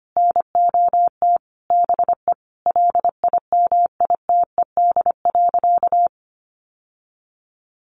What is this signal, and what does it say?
Morse code, heard as a keyed tone, about 25 wpm, 700 Hz. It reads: NOT BE LIMITED.
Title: Sentences from Top 900 Words 25wpm